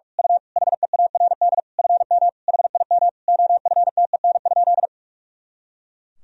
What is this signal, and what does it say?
Morse code, heard as a keyed tone, about 44 wpm, 700 Hz. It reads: U HEARD FM HIM OFTEN?
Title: Sentences from Top 400 Words and Abbreviations 45wpm